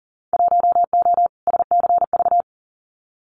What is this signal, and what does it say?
Morse code, heard as a keyed tone, about 40 wpm, 700 Hz. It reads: 1O SCV
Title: Winter Field Day Exchanges 40wpm